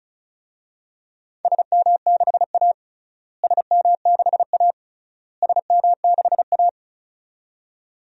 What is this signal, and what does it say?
Morse code, heard as a keyed tone, about 36 wpm, 700 Hz. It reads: SM6A SM6A SM6A
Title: DXCC-100 Call Signs - 4 Characters - Repeated 3x - 35wpm